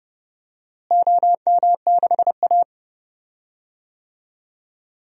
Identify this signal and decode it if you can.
Morse code, heard as a keyed tone, about 32 wpm, 700 Hz. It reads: OM6A
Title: DXCC-100 Call Signs - 4 Characters - Repeated 3x - 30wpm